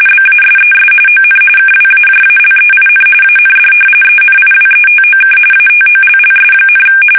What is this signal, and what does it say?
RTTY, about 45.45 baud. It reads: BASES ON KWAJALEIN AND ENIWETOK ATOLLS,
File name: fsk850 / battleshipnj